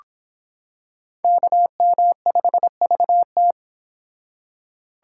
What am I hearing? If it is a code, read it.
Morse code, heard as a keyed tone, about 25 wpm, 700 Hz. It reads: KM5VT